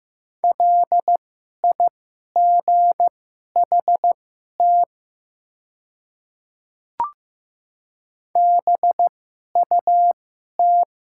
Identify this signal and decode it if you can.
Morse code, heard as a keyed tone, about 15 wpm, spaced as for 11 wpm, 700 Hz. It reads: LIGHT BUT
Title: 2x Character Spacing - Top 300 Words - 15wpm